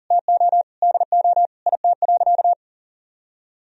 Morse code, heard as a keyed tone, about 40 wpm, 700 Hz. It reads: TO DO IT.
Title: Sentences from Top 1000 Words - Review Part 3 - 40wpm